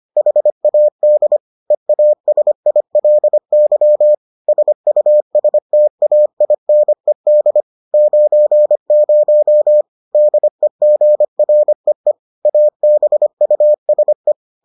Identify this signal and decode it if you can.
Morse code, heard as a keyed tone, about 25 wpm, 600 Hz. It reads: HAD EASILY SUSTAINED 90 DEGREE ABUSE